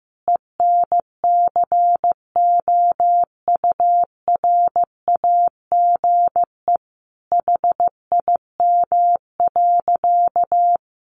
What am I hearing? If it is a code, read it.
Morse code, heard as a keyed tone, about 15 wpm, 700 Hz. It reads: ENCOURAGE HIM.
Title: Sentences from Top 1400 Words 15wpm